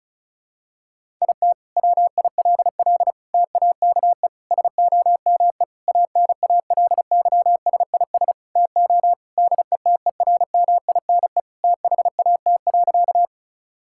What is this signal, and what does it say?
Morse code, heard as a keyed tone, about 36 wpm, 700 Hz. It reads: IT WILL TAKE SOME ANALYSIS TO DETERMINE THAT.
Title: Sentences from Top 1000 Words - Review Part 5 - 35wpm